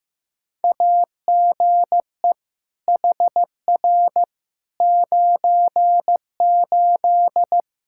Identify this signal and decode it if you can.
Morse code, heard as a keyed tone, about 15 wpm, 700 Hz. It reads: AGE HR 98
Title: K1USN SST QSO Practice 15wpm